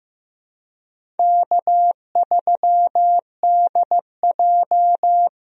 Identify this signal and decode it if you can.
Morse code, heard as a keyed tone, about 15 wpm, 700 Hz. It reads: K3DJ
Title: USA Call Signs - 4 Characters - Repeated 2x - 15wpm